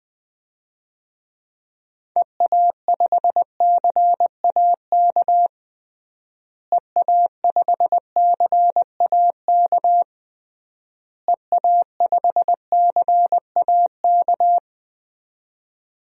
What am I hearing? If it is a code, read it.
Morse code, heard as a keyed tone, about 20 wpm, 700 Hz. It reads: EA5CAK EA5CAK EA5CAK
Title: DXCC-100 Call Signs - 6 Characters - Repeated 3x - 20wpm